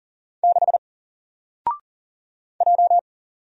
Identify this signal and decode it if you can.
Morse code, heard as a keyed tone, about 40 wpm, 700 Hz. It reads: 6 J
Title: Single Letter-Number 40wpm